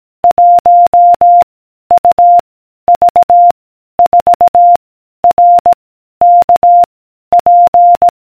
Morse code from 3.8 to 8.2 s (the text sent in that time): 4 R K P